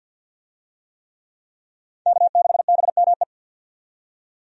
Morse code, heard as a keyed tone, about 50 wpm, 700 Hz. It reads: K6BME
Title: USA Call Signs - 5 Characters - 50wpm